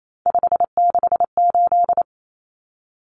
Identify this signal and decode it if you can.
Morse code, heard as a keyed tone, about 28 wpm, 700 Hz. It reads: 568
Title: Numbers - 3-Digit 28wpm